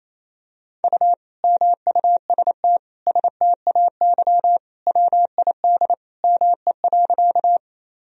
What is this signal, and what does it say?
Morse code, heard as a keyed tone, about 28 wpm, 700 Hz. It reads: U MUST STAY WID ME.
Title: Sentences from Top 500 Words and Abbreviations 28wpm